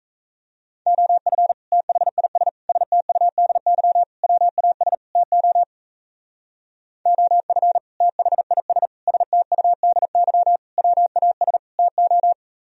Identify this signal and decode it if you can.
Morse code, heard as a keyed tone, about 37 wpm, 700 Hz. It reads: OF THIS STUDY WAS TO OF THIS STUDY WAS TO